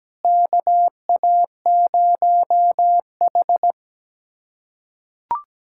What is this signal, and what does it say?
Morse code, heard as a keyed tone, about 17 wpm, 700 Hz. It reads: KA0H
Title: USA Callsign DE Callsign - 4 Characters - 17wpm